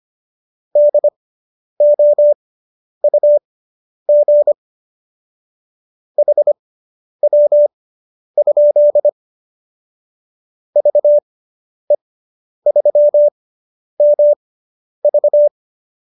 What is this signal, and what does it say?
Morse code, heard as a keyed tone, about 25 wpm, spaced as for 10 wpm, 590 Hz. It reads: DOUG HW? VE3MV